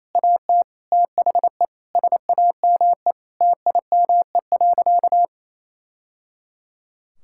Morse code, heard as a keyed tone, about 28 wpm, 700 Hz. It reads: AT THE SAME TIME.
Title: Sentences from Top 1000 Words - Review Part 6 - 28wpm